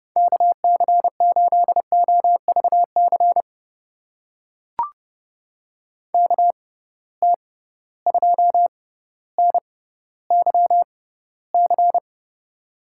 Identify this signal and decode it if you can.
Morse code, heard as a keyed tone, about 30 wpm, spaced as for 15 wpm, 700 Hz. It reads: KC8OVC KT2NYC
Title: USA Call Signs - 6 Characters - 6x Character Spacing - 30wpm